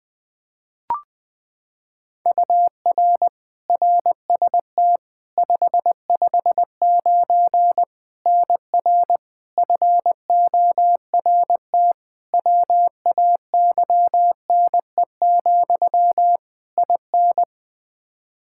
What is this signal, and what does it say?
Morse code, heard as a keyed tone, about 20 wpm, 700 Hz. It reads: UR RST 559 NR FORT WAYNE, IN
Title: Practice QSO - Exchange 1 - Beginning - 20wpm